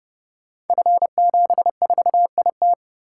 Morse code, heard as a keyed tone, about 28 wpm, 700 Hz. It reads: F74IT
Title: Code Groups - 5 Letter:Numbers - 30wpm